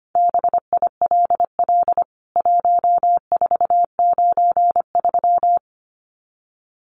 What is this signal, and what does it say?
Morse code, heard as a keyed tone, about 25 wpm, 700 Hz. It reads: BILL 1493
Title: CWT Exchange 25wpm